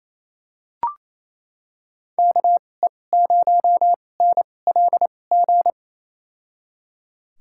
Morse code, heard as a keyed tone, about 28 wpm, spaced as for 20 wpm, 700 Hz. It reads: KE0NLG
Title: USA Call Signs - 6 Characters - 2x Character Spacing - 28wpm